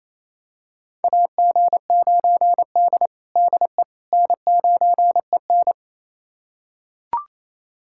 Morse code, heard as a keyed tone, about 28 wpm, 700 Hz. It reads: AG9D DE N9EN
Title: USA Callsign DE Callsign - 4 Characters - 28wpm